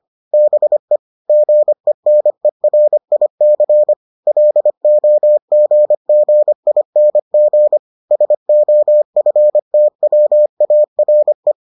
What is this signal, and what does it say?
Morse code, heard as a keyed tone, about 25 wpm, 600 Hz. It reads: BE GENERIC LOGGING SOFTWARE